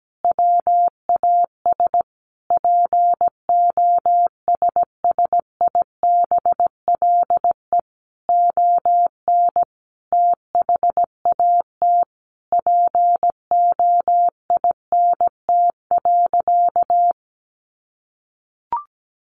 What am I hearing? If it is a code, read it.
Morse code, heard as a keyed tone, about 17 wpm, 700 Hz. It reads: WAS POSSIBLE ON THAT POINT.
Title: Sentences from Top 1400 Words 17wpm